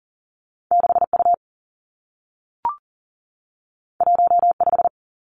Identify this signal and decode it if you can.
Morse code, heard as a keyed tone, about 40 wpm, 700 Hz. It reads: 6U 15
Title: Two Challenging Letter-Number Combinations 40wpm